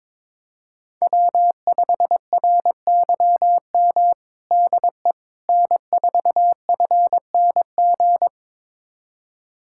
Morse code, heard as a keyed tone, about 22 wpm, 700 Hz. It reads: W5RYM DE N4FNG
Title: USA Callsign DE Callsign - 5 Characters - 22wpm